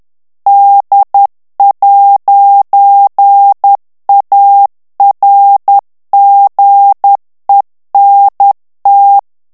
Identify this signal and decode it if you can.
Morse code, heard as a keyed tone, about 10 wpm, 800 Hz. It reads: D'ARGENT
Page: Français - texte en clair 1 - vitesse : 10 mots minute